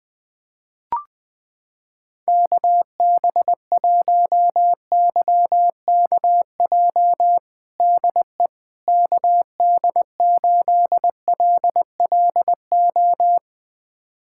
Morse code, heard as a keyed tone, about 20 wpm, 700 Hz. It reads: KB1YKJ DE KD8LLO
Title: USA Callsign DE Callsign - 6 Characters - 20wpm